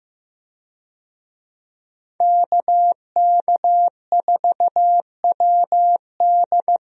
Morse code, heard as a keyed tone, about 15 wpm, 700 Hz. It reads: KK4WD
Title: USA Call Signs - 5 Characters - 15wpm